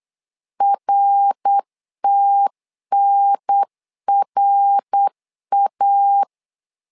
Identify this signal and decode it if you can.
Morse code, heard as a keyed tone, about 8 wpm, 800 Hz. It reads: RTNRA
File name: LEZIONI / 3 MWRO / 3 MWR 8wpm 2444